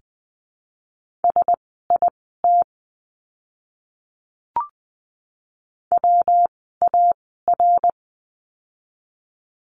Morse code, heard as a keyed tone, about 20 wpm, spaced as for 14 wpm, 700 Hz. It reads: SIT WAR
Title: 115 Lesson 19f - Letter w - Words 20wpm